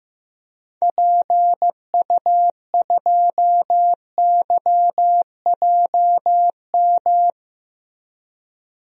Morse code, heard as a keyed tone, about 15 wpm, 700 Hz. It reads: PU2YJM
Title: DXCC-100 Call Signs - 6 Characters - 15wpm